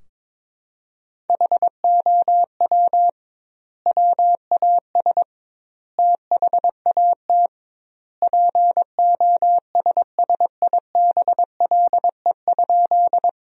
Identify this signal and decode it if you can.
Morse code, heard as a keyed tone, about 22 wpm, 700 Hz. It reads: HOW WAS THAT POSSIBLE?
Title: 2x Word-Spacing - Sentences from Top 500 Words - 22wpm